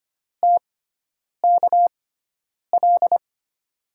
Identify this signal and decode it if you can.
Morse code, heard as a keyed tone, about 25 wpm, spaced as for 9 wpm, 700 Hz. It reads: TKL
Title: USA Call Signs - 6x Character Spacing - 25wpm